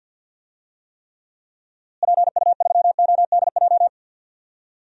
Morse code, heard as a keyed tone, about 52 wpm, 700 Hz. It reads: PU3ODJ
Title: DXCC-100 Call Signs - 6 Characters - Repeated 3x - 50wpm